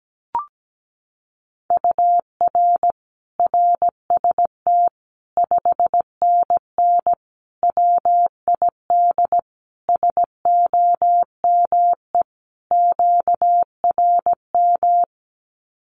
Morse code, heard as a keyed tone, about 17 wpm, 700 Hz. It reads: UR RST 5NN WID SOME QRM
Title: Common QSO Phrases 17wpm